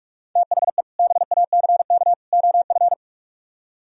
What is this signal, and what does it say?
Morse code, heard as a keyed tone, about 49 wpm, 700 Hz. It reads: THE BACK OF